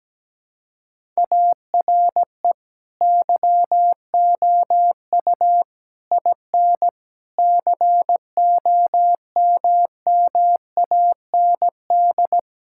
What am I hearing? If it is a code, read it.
Morse code, heard as a keyed tone, about 17 wpm, 700 Hz. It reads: ARE YOU IN COMMAND
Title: Sentences from Top 1600 Words 17wpm